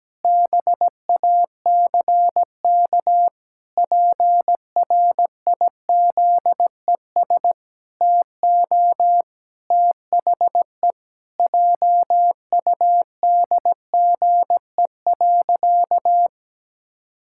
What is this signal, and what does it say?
Morse code, heard as a keyed tone, about 17 wpm, 700 Hz. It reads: BACK PRIZES TO THE JUDGE.
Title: Pangram Challenge v2 - 17wpm